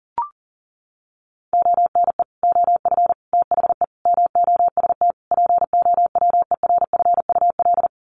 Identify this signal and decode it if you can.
Morse code, heard as a keyed tone, about 40 wpm, 700 Hz. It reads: ONE OF THE MOST POWERFUL
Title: Sets of 5 Words Encore 40wpm